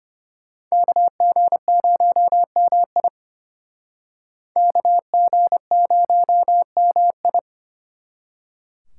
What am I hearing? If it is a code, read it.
Morse code, heard as a keyed tone, about 32 wpm, 700 Hz. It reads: KG0MI KG0MI